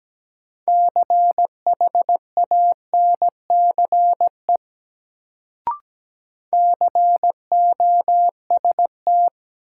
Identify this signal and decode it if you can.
Morse code, heard as a keyed tone, about 17 wpm, 700 Hz. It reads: CHANCE COST
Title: Sets of 3 Words - New Word Drill - 17wpm